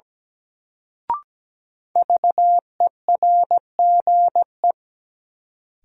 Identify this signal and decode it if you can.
Morse code, heard as a keyed tone, about 17 wpm, 700 Hz. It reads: VERGE
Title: Sets of 4 Words - New Word Drill - 17wpm